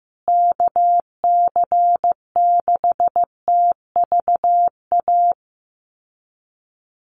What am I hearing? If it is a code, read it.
Morse code, heard as a keyed tone, about 15 wpm, 700 Hz. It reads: KC6TVA
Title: USA Call Signs Repeated 3x 15wpm